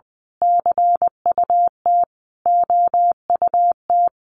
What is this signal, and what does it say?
Morse code, heard as a keyed tone, about 20 wpm, 700 Hz. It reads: CUT OUT
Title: A Sound Of Thunder 20wpm